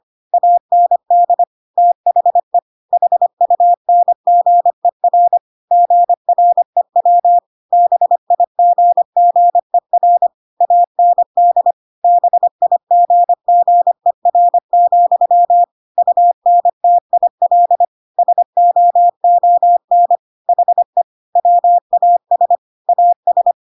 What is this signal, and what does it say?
Morse code, heard as a keyed tone, about 25 wpm, 700 Hz. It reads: AND THE HUNGER GREW BIGGER AND BIGGER, UNTIL SOON HE WAS AS
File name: Pinocchio / Pinocchio ch5 25 25 ik1hge